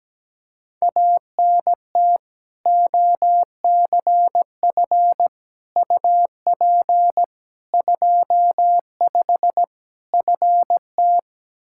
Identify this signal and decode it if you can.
Morse code, heard as a keyed tone, about 17 wpm, 700 Hz. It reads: ANT OCF UP 25 FT
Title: Practice QSO - Exchange 2 - Info - 17wpm